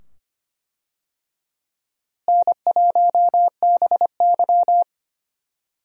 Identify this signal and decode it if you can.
Morse code, heard as a keyed tone, about 25 wpm, 700 Hz. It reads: N1BY
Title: USA Call Signs - 4 Characters - Repeated 2x - 25wpm